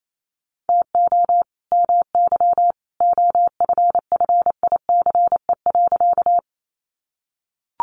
Morse code, heard as a keyed tone, about 28 wpm, 700 Hz. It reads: TO MY OFFICE.
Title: Sentences from Top 1000 Words 28wpm